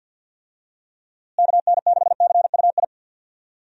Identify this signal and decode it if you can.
Morse code, heard as a keyed tone, about 47 wpm, 700 Hz. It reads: KN6CUI
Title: USA Call Signs - 6 Characters - Repeated 3x - 50wpm